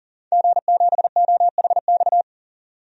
Morse code, heard as a keyed tone, about 40 wpm, 700 Hz. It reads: G7OHX
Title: DXCC-100 Call Signs - 5 Characters - 40wpm